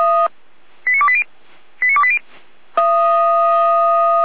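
Five-tone selective call (ZVEI), heard as quick runs of five tones, tones of about 70 ms. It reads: 89280 89280
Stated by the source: ZVEI five-tone call 89262 received by SDR, fire brigade Feuerwehr Orlamünde.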